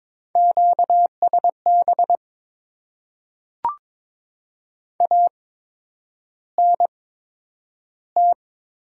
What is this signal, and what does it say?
Morse code, heard as a keyed tone, about 21 wpm, spaced as for 9 wpm, 700 Hz. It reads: QSB ANT
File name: Top 90 QSO Elements - 8x Character Spacing - 22wpm